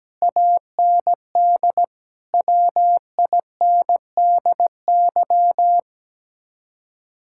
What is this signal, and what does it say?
Morse code, heard as a keyed tone, about 17 wpm, 700 Hz. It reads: AND WINDY
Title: Speed-Racing Common QSO Phrases Encore - 17wpm